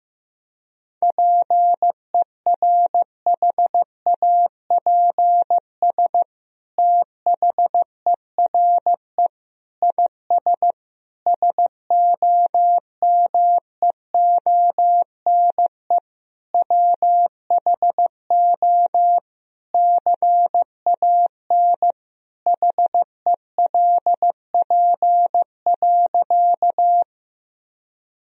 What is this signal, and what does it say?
Morse code, heard as a keyed tone, about 15 wpm, 700 Hz. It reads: PERHAPS THERE IS SOMEONE WHO CAN HELP.
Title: Sentences from Top 1000 Words - Review Part 4 - 15wpm